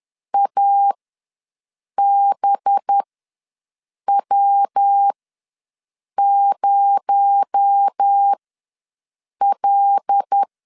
Morse code, heard as a keyed tone, about 10 wpm, spaced as for 6 wpm, 800 Hz. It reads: ABW0L
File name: LEZIONI / 7 GF9Z / 7 GF ws 10wpm 3918